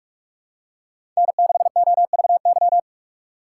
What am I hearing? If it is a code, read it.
Morse code, heard as a keyed tone, about 44 wpm, 700 Hz. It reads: N6OVY
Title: USA Call Signs - 5 Characters - 2x Character Spacing - 45wpm